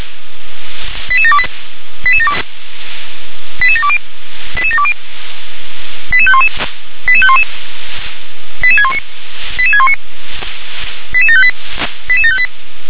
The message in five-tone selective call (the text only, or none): 88728 88720 88720 88511 88511 88619 88619 88758 88758